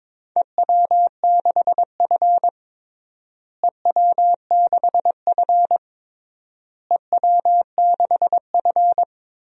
Morse code, heard as a keyed tone, about 21 wpm, 700 Hz. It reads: EW6F EW6F EW6F
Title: DXCC-100 Call Signs - 4 Characters - Repeated 3x - 22wpm